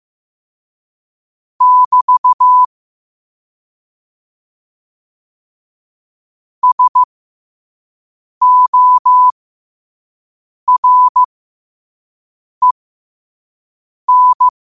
Morse code, heard as a keyed tone, about 15 wpm, spaced as for 5 wpm, 1000 Hz. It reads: <BT> SOREN
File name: qotdfiles / 5wpm